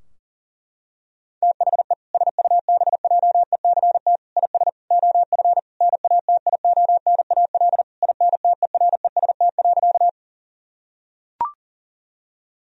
Morse code, heard as a keyed tone, about 40 wpm, 700 Hz. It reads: THE SUBJECT IS OF NATIONAL INTEREST.
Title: Sentences from Top 1000 Words - Review Part 4 - 40wpm